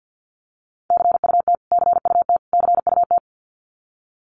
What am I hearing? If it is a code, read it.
Morse code, heard as a keyed tone, about 49 wpm, 700 Hz. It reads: CUT CUT CUT
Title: Top 500 Words Repeated 50wpm